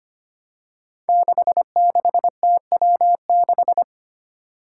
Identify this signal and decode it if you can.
Morse code, heard as a keyed tone, about 25 wpm, 700 Hz. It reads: 66TW6
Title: Five Challenging Letter-Number Combinations 25wpm